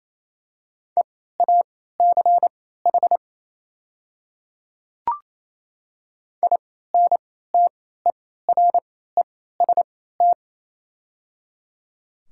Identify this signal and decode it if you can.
Morse code, heard as a keyed tone, about 27 wpm, spaced as for 16 wpm, 690 Hz. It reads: EACH INTEREST
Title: 3x Character Spacing - Top 500 Words - 28wpm